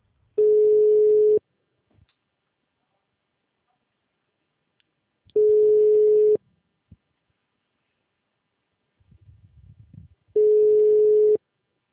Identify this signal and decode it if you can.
Ringback tone (Europe, 425 Hz).